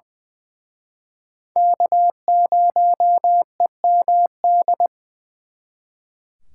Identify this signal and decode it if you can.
Morse code, heard as a keyed tone, about 23 wpm, 700 Hz. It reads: K0EMD
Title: Speed Racing - USA 1x3 Call Signs 20wpm